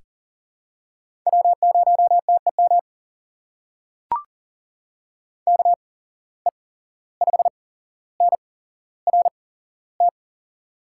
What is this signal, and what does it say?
Morse code, heard as a keyed tone, about 40 wpm, spaced as for 16 wpm, 700 Hz. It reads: W0TEM KE5NRT
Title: USA Call Signs - 8x Character Spacing - 40wpm